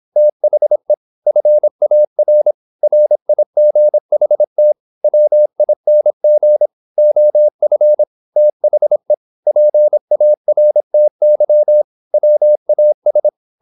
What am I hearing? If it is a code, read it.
Morse code, heard as a keyed tone, about 26 wpm, 600 Hz. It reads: THE FAR RIGHT WING OF THE PARTY WAS
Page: Never 26wpm | CW med Gnister